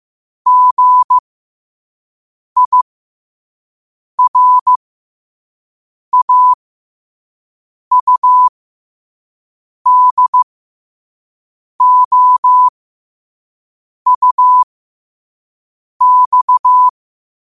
Morse code, heard as a keyed tone, about 15 wpm, spaced as for 5 wpm, 1000 Hz. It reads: GIRAUDOUX
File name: qotdfiles / 5wpm